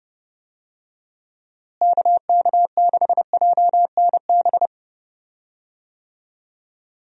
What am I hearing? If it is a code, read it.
Morse code, heard as a keyed tone, about 31 wpm, 700 Hz. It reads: KK6JNB
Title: USA Call Signs - 6 Characters - Repeated 3x - 30wpm